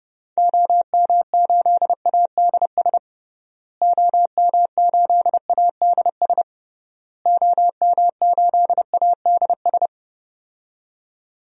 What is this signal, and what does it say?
Morse code, heard as a keyed tone, about 30 wpm, 700 Hz. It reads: OM8ADS OM8ADS OM8ADS
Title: DXCC-100 Call Signs - 6 Characters - Repeated 3x - 30wpm